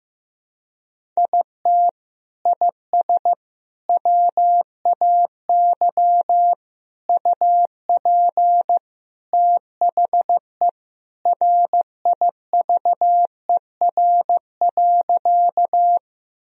Morse code, heard as a keyed tone, about 15 wpm, 700 Hz. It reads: IT IS WAY UP THE RIVER.
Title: Sentences from Top 1000 Words - Review Part 1 - 15wpm